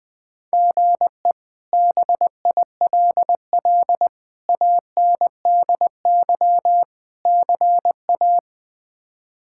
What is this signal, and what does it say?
Morse code, heard as a keyed tone, about 21 wpm, 700 Hz. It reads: GE BILL ANDY CA
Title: K1USN SST QSO Practice - 20wpm